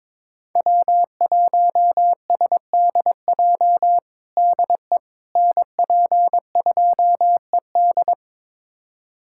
Morse code, heard as a keyed tone, about 22 wpm, 700 Hz. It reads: W1SDJ DE NP2ED
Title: USA Callsign DE Callsign - 5 Characters - 22wpm